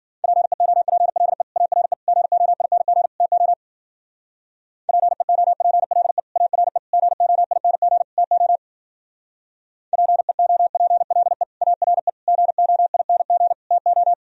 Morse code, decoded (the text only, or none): PEOPLE ARE GOING TO PEOPLE ARE GOING TO PEOPLE ARE GOING TO